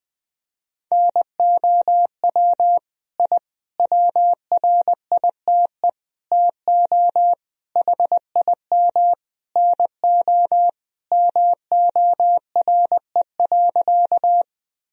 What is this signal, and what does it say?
Morse code, heard as a keyed tone, about 20 wpm, 700 Hz. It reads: NOW I WRITE TO HIM NO MORE.
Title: Sentences from Top 1000 Words - Review Part 1 - 20wpm